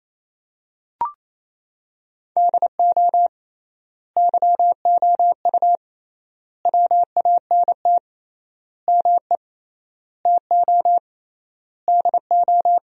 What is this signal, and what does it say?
Morse code, heard as a keyed tone, about 28 wpm, 700 Hz. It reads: DO YOU WANT ME TO DO
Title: 3x Word-Spacing - Sentences from Top 200 Words - 28wpm